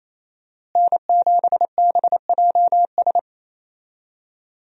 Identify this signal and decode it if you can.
Morse code, heard as a keyed tone, about 28 wpm, 700 Hz. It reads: N7BJS
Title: USA Call Signs - 5 Characters - Repeated 2x - 28wpm